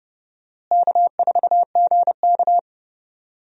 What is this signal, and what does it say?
Morse code, heard as a keyed tone, about 29 wpm, 700 Hz. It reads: K4GK
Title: USA Call Signs - 4 Characters - Repeated 3x - 30wpm